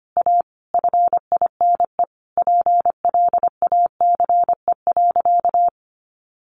Morse code, heard as a keyed tone, about 25 wpm, 700 Hz. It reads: A FINE PLACE.
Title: Sentences from Top 600 Words 25wpm